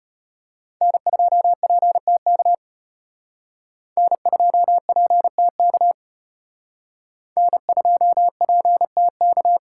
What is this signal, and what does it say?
Morse code, heard as a keyed tone, about 33 wpm, 700 Hz. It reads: N2PTK N2PTK N2PTK